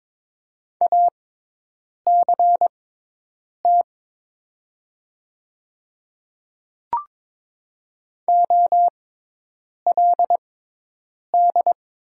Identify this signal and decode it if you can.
Morse code, heard as a keyed tone, about 22 wpm, spaced as for 8 wpm, 690 Hz. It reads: ACT OLD
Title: 6x Character Spacing - Top 200 Words - 22wpm